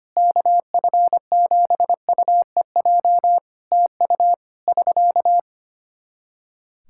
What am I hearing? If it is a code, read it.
Morse code, heard as a keyed tone, about 25 wpm, 700 Hz. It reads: KF7UEJ TU <SK>
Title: K1USN SST QSO Practice 25wpm